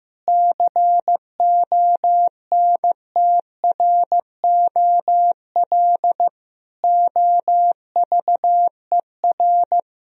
Morse code, heard as a keyed tone, about 16 wpm, 700 Hz. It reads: CONTROL OVER
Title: Sentences from Top 1000 Words - Review Part 4 - 15wpm